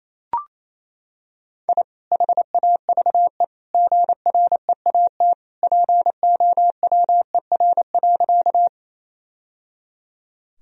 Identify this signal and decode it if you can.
Morse code, heard as a keyed tone, about 28 wpm, 700 Hz. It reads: I HAVE GREAT POWER.
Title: Sentences from Top 1000 Words - Review Part 2 - 28wpm